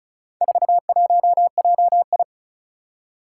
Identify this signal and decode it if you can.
Morse code, heard as a keyed tone, about 36 wpm, 700 Hz. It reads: 41JI
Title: Four Challenging Letter-Number Combinations 35wpm